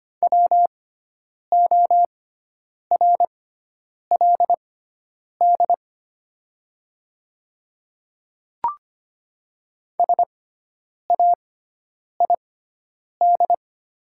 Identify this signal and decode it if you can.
Morse code, heard as a keyed tone, about 25 wpm, spaced as for 9 wpm, 700 Hz. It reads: WORLD SAID
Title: 6x Character Spacing - Top 300 Words - 25wpm